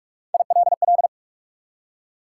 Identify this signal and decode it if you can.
Morse code, heard as a keyed tone, about 47 wpm, 700 Hz. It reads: ILL
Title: Three Letter Words - 45wpm